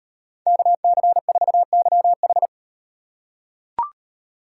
Morse code, heard as a keyed tone, about 35 wpm, 700 Hz. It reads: KC4YH